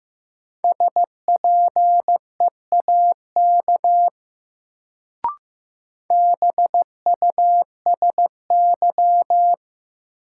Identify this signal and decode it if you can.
Morse code, heard as a keyed tone, about 15 wpm, 700 Hz. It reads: SPEAK BUSY
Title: Sentences from Top 1000 Words - Review Part 2 - New Word Drill - 15wpm